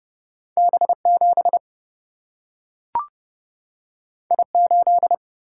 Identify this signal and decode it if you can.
Morse code, heard as a keyed tone, about 30 wpm, 700 Hz. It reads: B 7 I 8
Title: Single-Letter Single-Number 30wpm